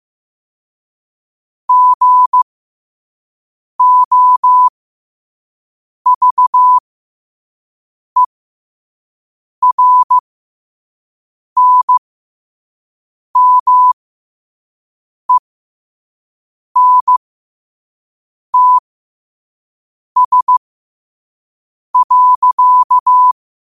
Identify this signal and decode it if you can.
Morse code, heard as a keyed tone, about 15 wpm, spaced as for 5 wpm, 1000 Hz. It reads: GOVERNMENTS.
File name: qotdfiles / 5wpm